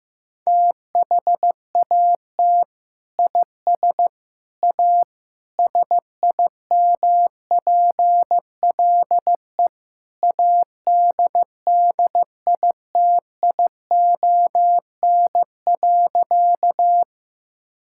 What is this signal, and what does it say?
Morse code, heard as a keyed tone, about 15 wpm, 700 Hz. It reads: THAT IS A SIMPLE ADDITION.
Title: Sentences from Top 1000 Words - Review Part 6 - 15wpm